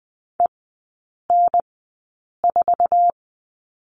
Morse code, heard as a keyed tone, about 20 wpm, spaced as for 8 wpm, 700 Hz. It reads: EN4
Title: 067 Lesson 12e - Letter d - Three Characters 20wpm